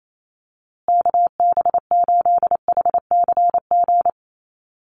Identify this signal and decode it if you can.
Morse code, heard as a keyed tone, about 28 wpm, 700 Hz. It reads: KB8HCG
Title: USA Call Signs Repeated 2x 28wpm